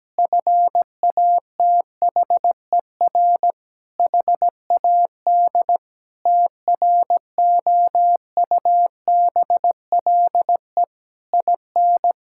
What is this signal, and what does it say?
Morse code, heard as a keyed tone, about 17 wpm, 700 Hz. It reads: FATHER HAD TROUBLE IN
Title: Sentences from Top 1600 Words 17wpm